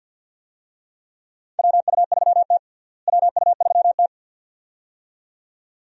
Morse code, heard as a keyed tone, about 50 wpm, 700 Hz. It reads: WU3T WU3T
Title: USA Call Signs - 4 Characters - Repeated 2x - 50wpm